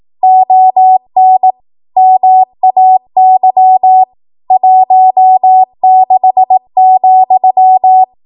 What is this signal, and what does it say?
Morse code, heard as a keyed tone, about 18 wpm, 750 Hz. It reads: ON MAY 16,
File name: Morse / 18WPM